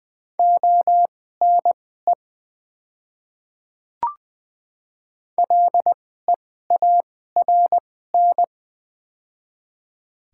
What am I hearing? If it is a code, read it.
Morse code, heard as a keyed tone, about 20 wpm, spaced as for 15 wpm, 700 Hz. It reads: ONE LEARN
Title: 2x Character Spacing - Top 300 Words - 20wpm